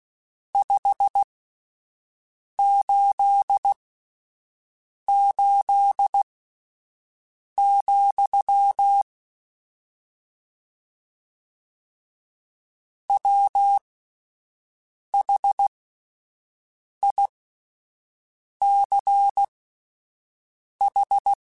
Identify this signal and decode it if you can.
Morse code, heard as a keyed tone, about 16 wpm, spaced as for 5 wpm, 800 Hz. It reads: 588, WHICH
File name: (spaced) / QSOs Spaced-mp3s / WA7QOX 1 @10 sp